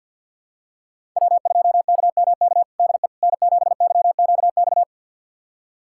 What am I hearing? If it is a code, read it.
Morse code, heard as a keyed tone, about 52 wpm, 700 Hz. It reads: W2KMK DE N7YQX
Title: USA Callsign DE Callsign - 5 Characters - 50wpm